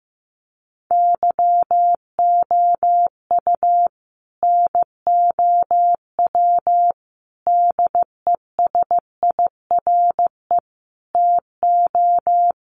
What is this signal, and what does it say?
Morse code, heard as a keyed tone, about 15 wpm, 700 Hz. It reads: YOU NOW DESIRE TO
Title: Sentences from Top 1100 Words 15wpm